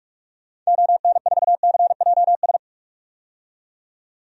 Morse code, heard as a keyed tone, about 46 wpm, 700 Hz. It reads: ON4CJS
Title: DXCC-100 Call Signs - 6 Characters - 45wpm